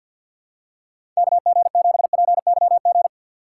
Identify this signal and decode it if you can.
Morse code, heard as a keyed tone, about 52 wpm, 700 Hz. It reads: KG7PYG
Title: USA Call Signs - 6x Character Spacing - 50wpm